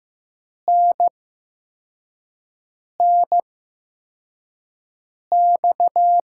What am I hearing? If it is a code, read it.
Morse code, heard as a keyed tone, about 15 wpm, spaced as for 4 wpm, 700 Hz. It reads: NNX
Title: USA Call Signs - 6 Characters - 8x Character Spacing - 15wpm